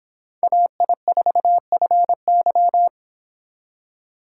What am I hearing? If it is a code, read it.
Morse code, heard as a keyed tone, about 26 wpm, 700 Hz. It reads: AI4FY